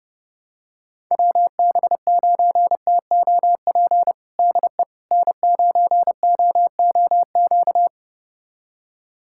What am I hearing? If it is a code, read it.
Morse code, heard as a keyed tone, about 32 wpm, 700 Hz. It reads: WB9TOP DE N9OOQ
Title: USA Callsign DE Callsign 30wpm